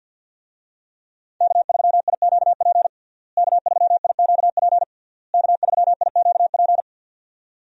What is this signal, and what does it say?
Morse code, heard as a keyed tone, about 49 wpm, 700 Hz. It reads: K3IQP K3IQP K3IQP
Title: USA Call Signs - 5 Characters - Repeated 3x - 50wpm